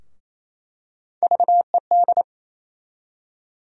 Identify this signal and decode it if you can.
Morse code, heard as a keyed tone, about 28 wpm, 700 Hz. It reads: VED
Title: USA Call Sign Postfixes - 3 Letters 28wpm